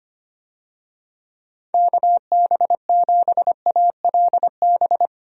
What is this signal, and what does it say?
Morse code, heard as a keyed tone, about 25 wpm, 700 Hz. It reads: KB7ALB
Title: USA Call Signs - 6 Characters - 25wpm